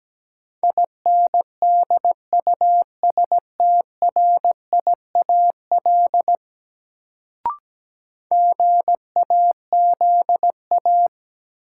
Morse code, heard as a keyed tone, about 17 wpm, 700 Hz. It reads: INDUSTRIAL GAZA
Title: Sets of 5 Words - New Word Drill - 17wpm